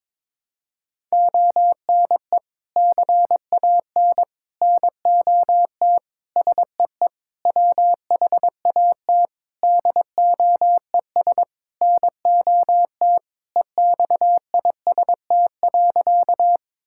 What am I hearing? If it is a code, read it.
Morse code, heard as a keyed tone, about 22 wpm, 700 Hz. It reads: ONE CAN NOT SEE WHAT DOES NOT EXIST.
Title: Sentences from Top 1000 Words - Review Part 5 - 22wpm